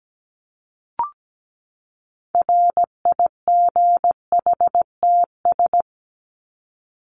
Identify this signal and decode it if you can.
Morse code, heard as a keyed tone, about 17 wpm, 700 Hz. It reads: RIGHTS
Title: ES:S Suffix - Top 100 Words with Base Words 17wpm